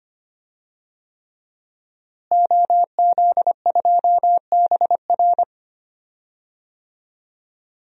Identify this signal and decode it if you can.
Morse code, heard as a keyed tone, about 25 wpm, 700 Hz. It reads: OZ2BR
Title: DXCC-100 Call Signs - 5 Characters - Repeated 3x - 25wpm